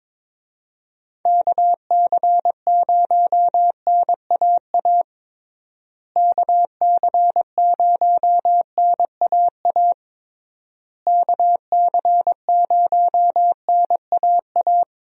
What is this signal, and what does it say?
Morse code, heard as a keyed tone, about 22 wpm, 700 Hz. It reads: KC0NAA KC0NAA KC0NAA
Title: USA Call Signs - 6 Characters - Repeated 3x - 22wpm